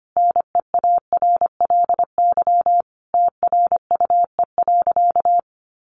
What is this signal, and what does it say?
Morse code, heard as a keyed tone, about 25 wpm, 700 Hz. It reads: NEARLY TRUE.
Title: Sentences from Top 1300 Words 25wpm